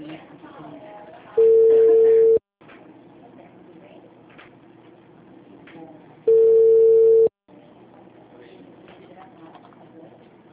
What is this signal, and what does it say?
Ringback tone (China, 450 Hz), with speech.
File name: out- -2030- - - .17444